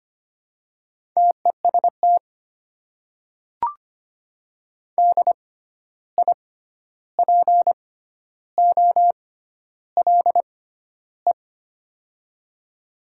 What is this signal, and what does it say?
Morse code, heard as a keyed tone, about 25 wpm, spaced as for 11 wpm, 700 Hz. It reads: TEST DIPOLE
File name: Top 90 QSO Elements - 6x Character Spacing - 25wpm